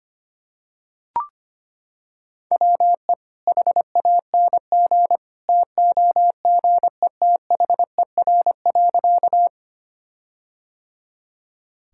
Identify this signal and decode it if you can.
Morse code, heard as a keyed tone, about 25 wpm, 700 Hz. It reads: WE HANG TOGETHER.
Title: Sentences from Top 1600 Words 25wpm